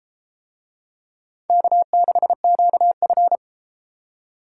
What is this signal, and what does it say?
Morse code, heard as a keyed tone, about 32 wpm, 700 Hz. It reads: K6QF